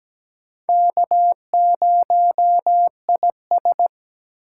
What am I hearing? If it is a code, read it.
Morse code, heard as a keyed tone, about 17 wpm, 700 Hz. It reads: K0IS
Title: USA Call Signs - 4 Characters - 2x Character Spacing - 17wpm